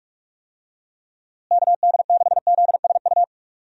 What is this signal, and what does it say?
Morse code, heard as a keyed tone, about 47 wpm, 700 Hz. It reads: KD6ZSU
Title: USA Call Signs - 6 Characters - Repeated 2x - 45wpm